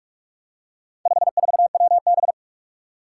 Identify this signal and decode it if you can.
Morse code, heard as a keyed tone, about 46 wpm, 700 Hz. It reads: 54WB
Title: Four Challenging Letter-Number Combinations 45wpm